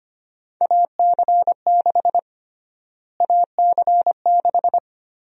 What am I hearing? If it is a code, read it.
Morse code, heard as a keyed tone, about 25 wpm, 700 Hz. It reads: AC6 AC6
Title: USA Call Sign Prefixes - Repeated 2x - 25wpm